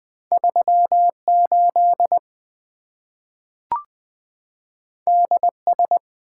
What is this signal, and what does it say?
Morse code, heard as a keyed tone, about 20 wpm, 700 Hz. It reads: 3 8 D S
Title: Two Letter-Number Combinations - 20wpm